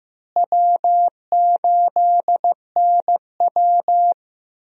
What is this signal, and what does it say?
Morse code, heard as a keyed tone, about 15 wpm, 700 Hz. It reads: W8NW
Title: USA Call Signs - 4 Characters - Repeated 3x - 15wpm